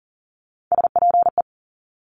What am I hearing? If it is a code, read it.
Morse code, heard as a keyed tone, about 39 wpm, 690 Hz. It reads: SPE
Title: Top Three Letter Combinations in Words 40wpm